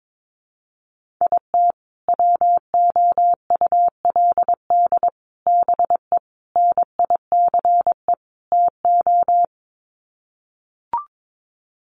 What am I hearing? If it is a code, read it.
Morse code, heard as a keyed tone, about 22 wpm, 700 Hz. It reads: IT WOULD BE NICE TO
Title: Sets of 5 Words Encore2 22wpm